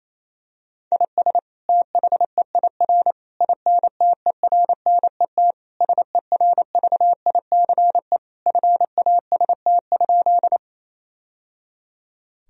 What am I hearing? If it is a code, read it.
Morse code, heard as a keyed tone, about 28 wpm, 700 Hz. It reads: IS THEIR INTERNET SERVICE FAST?
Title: Sentences from Top 1000 Words - Review Part 4 - 28wpm